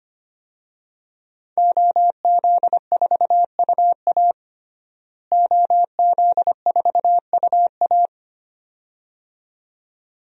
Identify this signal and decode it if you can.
Morse code, heard as a keyed tone, about 25 wpm, 700 Hz. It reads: OZ4UA OZ4UA
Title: DXCC-100 Call Signs - 5 Characters - Repeated 2x - 25wpm